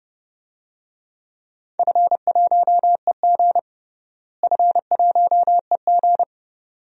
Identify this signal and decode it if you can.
Morse code, heard as a keyed tone, about 30 wpm, 700 Hz. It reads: F1EG F1EG
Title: DXCC-100 Call Signs - 4 Characters - Repeated 2x - 30wpm